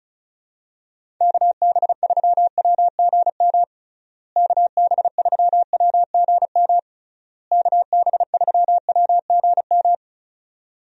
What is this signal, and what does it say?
Morse code, heard as a keyed tone, about 34 wpm, 700 Hz. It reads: KB3WGM KB3WGM KB3WGM
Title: USA Call Signs - 6 Characters - Repeated 3x - 35wpm